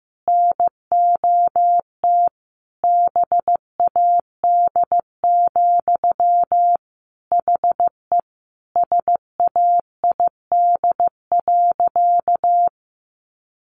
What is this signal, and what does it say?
Morse code, heard as a keyed tone, about 15 wpm, 700 Hz. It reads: NOT BAD, HE SAID.
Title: Sentences from Top 700 Words 15wpm